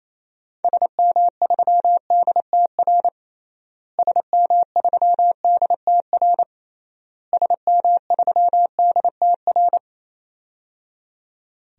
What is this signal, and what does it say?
Morse code, heard as a keyed tone, about 28 wpm, 700 Hz. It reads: SM3DTR SM3DTR SM3DTR
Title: DXCC-100 Call Signs - 6 Characters - Repeated 3x - 28wpm